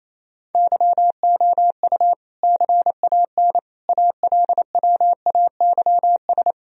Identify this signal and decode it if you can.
Morse code, heard as a keyed tone, about 28 wpm, 700 Hz. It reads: YOU CAN ALWAYS
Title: Sentences from Top 1000 Words - Review Part 4 - 28wpm